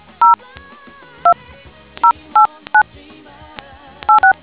DTMF, heard as telephone touch tones, tones of about 110 ms. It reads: 02089 86